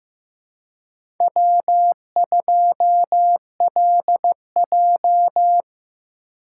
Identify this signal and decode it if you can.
Morse code, heard as a keyed tone, about 15 wpm, 700 Hz. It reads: W2LJ
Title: USA Call Signs - 4 Characters - 2x Character Spacing - 15wpm